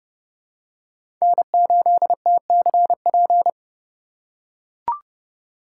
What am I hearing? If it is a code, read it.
Morse code, heard as a keyed tone, about 30 wpm, 700 Hz. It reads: N8TCP
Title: USA Call Signs - 5 Characters - 2x Character Spacing - 30wpm